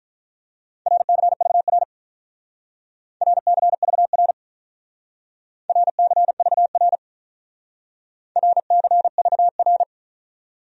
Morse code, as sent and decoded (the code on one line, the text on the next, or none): .-. -.-. ...- .-. / .-. -.-. ...- .-. / .-. -.-. ...- .-. / .-. -.-. ...- .-.
RCVR RCVR RCVR RCVR